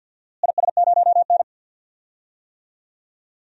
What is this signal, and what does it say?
Morse code, heard as a keyed tone, about 50 wpm, 700 Hz. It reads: IS0N
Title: DXCC-100 Call Signs - 4 Characters - Repeated 2x - 50wpm